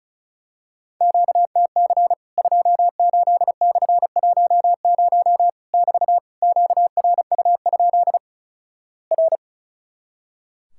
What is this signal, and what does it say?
Morse code, heard as a keyed tone, about 36 wpm, 700 Hz. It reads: QTC 28/10 <BT> QRU? R
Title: Worked All Europe DX Contest - QTC Exchange - 35wpm